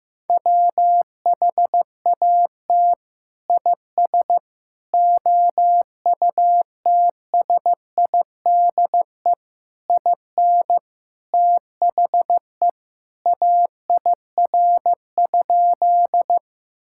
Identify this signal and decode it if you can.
Morse code, heard as a keyed tone, about 15 wpm, 700 Hz. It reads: WHAT IS OUTSIDE IN THE AIR?
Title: Sentences from Top 1000 Words - Review Part 6 - 15wpm